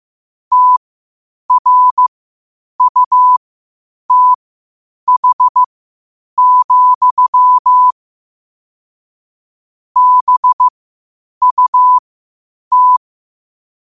Morse code, heard as a keyed tone, about 15 wpm, spaced as for 8 wpm, 1000 Hz. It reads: TRUTH, BUT
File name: qotdfiles / 8wpm